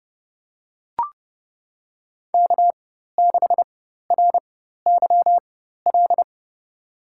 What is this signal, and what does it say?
Morse code, heard as a keyed tone, about 30 wpm, spaced as for 14 wpm, 700 Hz. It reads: K6RYL
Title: USA Call Signs - 5 Characters - 4x Character Spacing - 30wpm